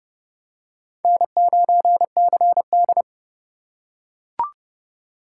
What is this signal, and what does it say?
Morse code, heard as a keyed tone, about 26 wpm, 700 Hz. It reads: N9CD